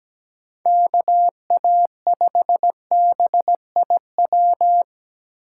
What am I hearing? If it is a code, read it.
Morse code, heard as a keyed tone, about 17 wpm, 700 Hz. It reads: KA5BIW
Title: USA Call Signs - 6 Characters - 8x Character Spacing - 17wpm